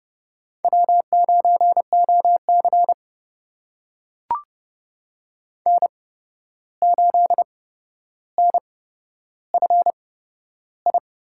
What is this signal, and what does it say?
Morse code, heard as a keyed tone, about 31 wpm, spaced as for 12 wpm, 700 Hz. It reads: W9OC N8NFI
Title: USA Call Signs - 8x Character Spacing - 30wpm